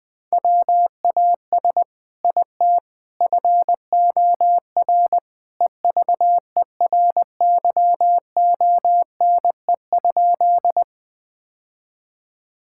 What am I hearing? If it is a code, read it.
Morse code, heard as a keyed tone, about 21 wpm, 700 Hz. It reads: WAS IT FOR EVERYONE?
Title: Sentences from Top 1000 Words - Review Part 5 - 20wpm